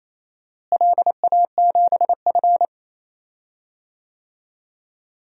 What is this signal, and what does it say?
Morse code, heard as a keyed tone, about 28 wpm, 700 Hz. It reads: LA7F
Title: DXCC-100 Call Signs - 4 Characters - Repeated 3x - 28wpm